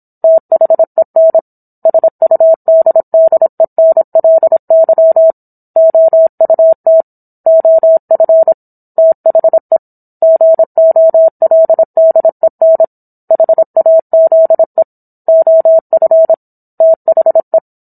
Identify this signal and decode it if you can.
Morse code, heard as a keyed tone, about 27 wpm, 650 Hz. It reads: THEN SUDDENLY OUT OF THE GOLDEN HAZE OF THE
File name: Book One / 17-Chapter 17 - 26 WPM